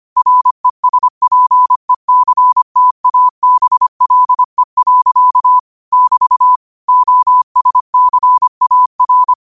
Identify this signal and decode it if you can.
Morse code, heard as a keyed tone, about 25 wpm, 1000 Hz. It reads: RESPECTABLE. <BT> OSCAR
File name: qotdfiles / 25wpm